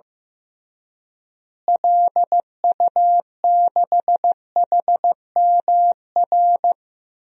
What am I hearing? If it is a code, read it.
Morse code, heard as a keyed tone, about 16 wpm, 700 Hz. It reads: LU6HMR
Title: DXCC-100 Call Signs - 6 Characters - Repeated 3x - 15wpm